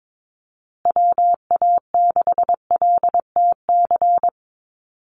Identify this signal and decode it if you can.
Morse code, heard as a keyed tone, about 22 wpm, 700 Hz. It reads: WA6LTC
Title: USA Call Signs Repeated 3x 22wpm